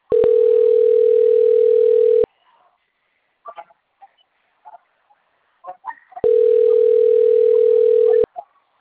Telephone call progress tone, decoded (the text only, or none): ringback tone